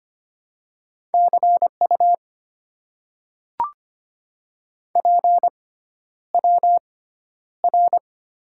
Morse code, heard as a keyed tone, about 25 wpm, spaced as for 11 wpm, 700 Hz. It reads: CU PWR
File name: Top 90 QSO Elements - 6x Character Spacing - 25wpm